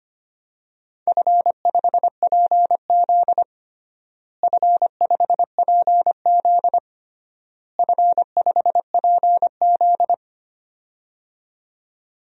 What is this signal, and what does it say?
Morse code, heard as a keyed tone, about 25 wpm, 700 Hz. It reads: F5PZ F5PZ F5PZ
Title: DXCC-100 Call Signs - 4 Characters - Repeated 3x - 25wpm